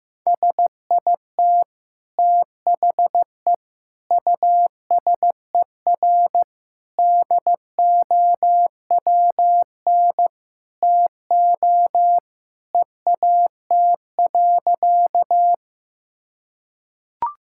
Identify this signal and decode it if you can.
Morse code, heard as a keyed tone, about 15 wpm, 700 Hz. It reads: SIT THE USER DOWN TO EAT.
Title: Sentences from Top 1000 Words - Review Part 5 - 15wpm